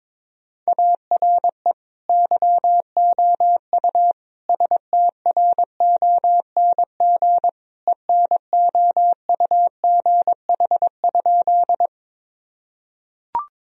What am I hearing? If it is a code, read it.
Morse code, heard as a keyed tone, about 22 wpm, 700 Hz. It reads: ARE YOU STRONG ENOUGH?
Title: Sentences from Top 1000 Words - Review Part 3 - 22wpm